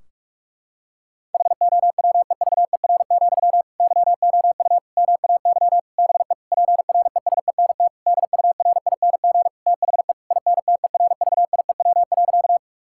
Morse code, heard as a keyed tone, about 46 wpm, 700 Hz. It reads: HOWEVER, YOU MAY BE PRESENT DURING THE INTERVIEW.
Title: Sentences from Top 1600 Words 45wpm